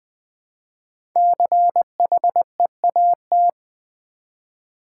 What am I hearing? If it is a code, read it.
Morse code, heard as a keyed tone, about 20 wpm, 700 Hz. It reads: CHEAT
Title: 5 Letter Words - First 3 Letters Given - 20wpm